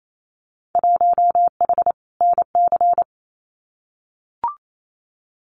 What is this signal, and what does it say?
Morse code, heard as a keyed tone, about 28 wpm, 700 Hz. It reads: 1H NC
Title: Winter Field Day Exchanges 28wpm